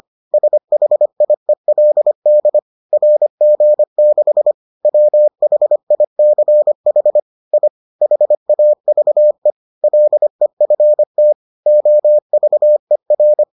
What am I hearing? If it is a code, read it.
Morse code, heard as a keyed tone, about 25 wpm, 600 Hz. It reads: SHIELD RG6 WHICH I HAVE LEFT OVER